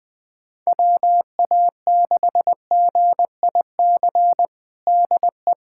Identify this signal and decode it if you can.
Morse code, heard as a keyed tone, about 20 wpm, 700 Hz. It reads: WA6GIC DE
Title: USA Callsign DE Callsign 20wpm